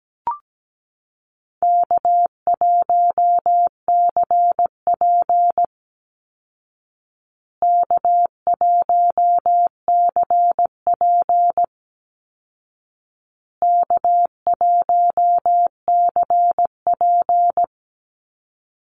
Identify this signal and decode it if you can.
Morse code, heard as a keyed tone, about 17 wpm, 700 Hz. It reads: K1CP K1CP K1CP
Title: Call Signs Repeated 2x - 17wpm